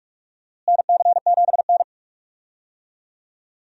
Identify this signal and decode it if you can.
Morse code, heard as a keyed tone, about 47 wpm, 700 Hz. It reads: NC7N
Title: USA Call Signs - 4 Characters - Repeated 3x - 45wpm